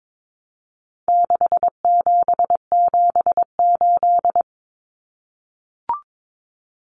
Morse code, heard as a keyed tone, about 22 wpm, 700 Hz. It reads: 6778
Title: Numbers - 4-Digit 22wpm